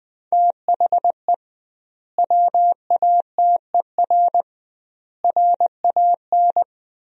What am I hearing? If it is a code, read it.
Morse code, heard as a keyed tone, about 20 wpm, 700 Hz. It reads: THE WATER RAN
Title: 2x Word-Spacing - Sentences from Top 500 Words - 20wpm